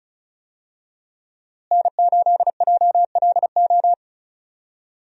Morse code, heard as a keyed tone, about 34 wpm, 700 Hz. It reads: N8JLO
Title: USA Call Signs - 5 Characters - Repeated 3x - 35wpm